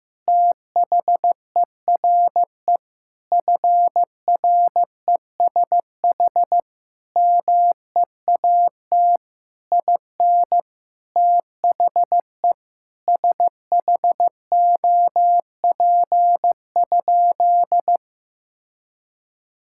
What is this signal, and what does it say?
Morse code, heard as a keyed tone, about 16 wpm, 700 Hz. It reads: THERE FRESH MEAT IN THE SHOP?
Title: Sentences from Top 1600 Words 15wpm